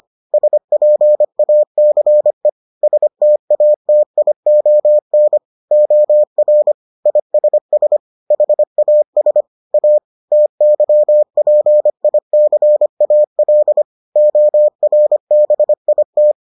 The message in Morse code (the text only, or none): SPACE STATION OR ISS HAS A TYPICAL ORBIT